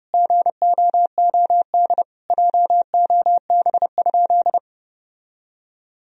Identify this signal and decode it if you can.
Morse code, heard as a keyed tone, about 30 wpm, 700 Hz. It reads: GOOD JOB?
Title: Sentences from Top 1000 Words - Review Part 3 - 30wpm